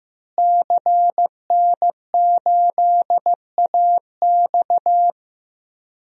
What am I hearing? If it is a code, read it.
Morse code, heard as a keyed tone, about 15 wpm, 700 Hz. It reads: CN8AX
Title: DXCC-100 Call Signs - 5 Characters - Repeated 2x - 15wpm